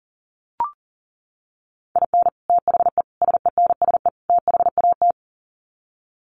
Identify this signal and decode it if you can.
Morse code, heard as a keyed tone, about 40 wpm, 700 Hz. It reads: IN THE SENSE THAT
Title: Sets of 4 Words 40wpm